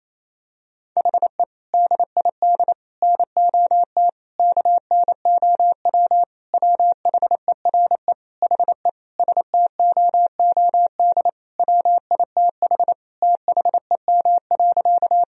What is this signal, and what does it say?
Morse code, heard as a keyed tone, about 28 wpm, 700 Hz. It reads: HE DID NOT KNOW WHERE HE STOOD WITH THEM.
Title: Sentences from Top 1000 Words - Review Part 2 - 28wpm